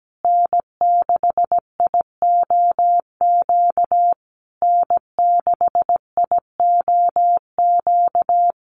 Morse code, heard as a keyed tone, about 17 wpm, 700 Hz. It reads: N6IOQ N6IOQ
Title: USA Call Signs Repeated 3x 17wpm